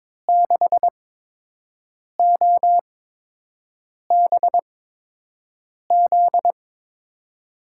Morse code, heard as a keyed tone, about 23 wpm, spaced as for 6 wpm, 700 Hz. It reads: 6OBZ
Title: USA Call Signs - 6 Characters - 8x Character Spacing - 22wpm